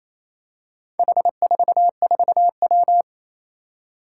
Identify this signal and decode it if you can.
Morse code, heard as a keyed tone, about 28 wpm, 700 Hz. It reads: H44W
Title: Four Challenging Letter-Number Combinations 28wpm